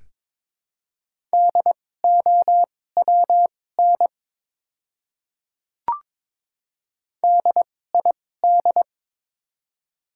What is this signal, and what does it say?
Morse code, heard as a keyed tone, about 22 wpm, spaced as for 16 wpm, 700 Hz. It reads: DOWN DID
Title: 2x Character Spacing - Top 200 Words - 22wpm